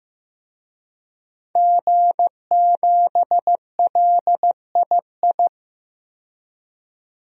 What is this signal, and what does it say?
Morse code, heard as a keyed tone, about 15 wpm, 700 Hz. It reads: G7LII
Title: DXCC-100 Call Signs - 5 Characters - 15wpm